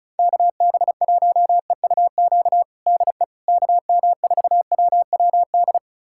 Morse code, heard as a keyed tone, about 34 wpm, 690 Hz. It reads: KB1EUQ DE KM4WWD
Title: USA Callsign DE Callsign - 6 Characters - 35wpm